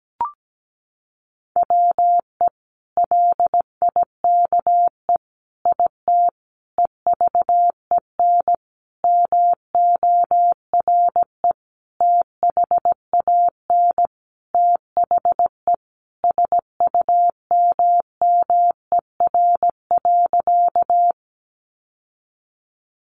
Morse code, heard as a keyed tone, about 17 wpm, 700 Hz. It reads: WE LIKE IT EVEN MORE THAN THE SUMMER.
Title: Sentences from Top 1100 Words 17wpm